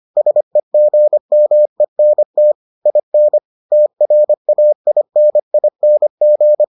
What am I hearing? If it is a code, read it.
Morse code, heard as a keyed tone, about 25 wpm, 600 Hz. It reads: SEGMENT IN TRAINING